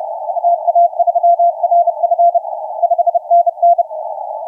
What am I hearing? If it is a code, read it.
Morse code, heard as a keyed tone, about 30 wpm, 690 Hz, with radio noise. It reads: TA3RF 5NN
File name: FS W6IZT 0934UTC 20m